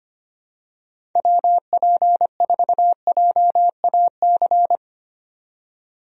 Morse code, heard as a keyed tone, about 25 wpm, 700 Hz. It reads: WP4JAC
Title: USA Call Signs - 6 Characters - 25wpm